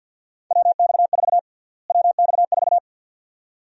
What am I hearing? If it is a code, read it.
Morse code, heard as a keyed tone, about 51 wpm, 700 Hz. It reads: WX4 WX4
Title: USA Call Sign Prefixes - Repeated 2x - 50wpm